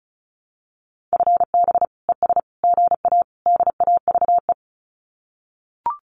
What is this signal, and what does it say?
Morse code, heard as a keyed tone, about 34 wpm, 700 Hz. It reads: FB ES GA DAVE
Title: Common QSO Phrases Encore 35wpm